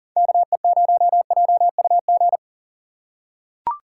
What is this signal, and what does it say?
Morse code, heard as a keyed tone, about 40 wpm, 700 Hz. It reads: KE0JUG
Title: USA Call Signs - 6 Characters - 3x Character Spacing - 40wpm